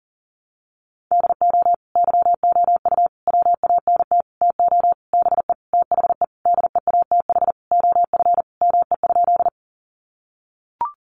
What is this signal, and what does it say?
Morse code, heard as a keyed tone, about 40 wpm, 700 Hz. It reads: DO YOU WANT TO BE THE DEATH OF ME?
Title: Sentences from Top 1100 Words 40wpm